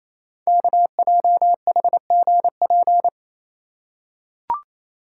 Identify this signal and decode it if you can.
Morse code, heard as a keyed tone, about 27 wpm, 700 Hz. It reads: KJHGP
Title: Code Groups - 5 Letters - 28wpm